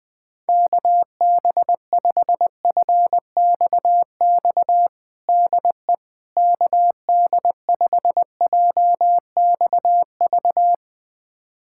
Morse code, heard as a keyed tone, about 20 wpm, 700 Hz. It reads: KB5FXX DE KD5JXV
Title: USA Callsign DE Callsign - 6 Characters - 20wpm